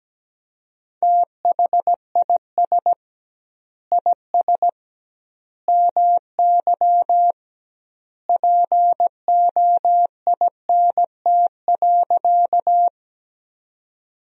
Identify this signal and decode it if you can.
Morse code, heard as a keyed tone, about 17 wpm, 700 Hz. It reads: THIS IS MY POINT.
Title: 2x Word-Spacing - Sentences from Top 200 Words - 17wpm